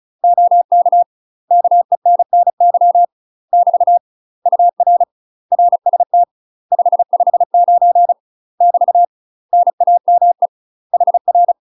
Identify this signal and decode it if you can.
Morse code, heard as a keyed tone, about 35 wpm, 700 Hz. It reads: OK KENNY <BT> UR RST 559 <BT> NAME HR
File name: UK MASTER QSOs TRAINING / 35WPM / UK MASTER QSOs 0042